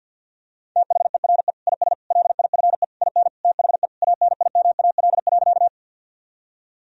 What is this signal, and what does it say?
Morse code, heard as a keyed tone, about 51 wpm, 700 Hz. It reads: THERE IS LIFE IN THE ANIMAL.
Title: Sentences from Top 1000 Words - Review Part 1 - 50wpm